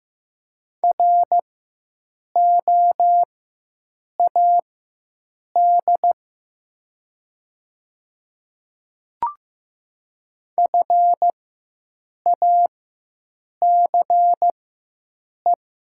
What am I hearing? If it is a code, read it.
Morse code, heard as a keyed tone, about 15 wpm, spaced as for 7 wpm, 700 Hz. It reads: ROAD FACE
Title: 4x Character Spacing - Top 500 Words - 15wpm